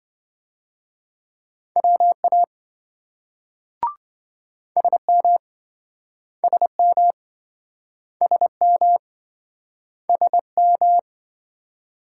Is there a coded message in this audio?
Yes: Morse code from 1.8 to 11.0 s, heard as a keyed tone, about 26 wpm, 700 Hz. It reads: WA SM SM SM SM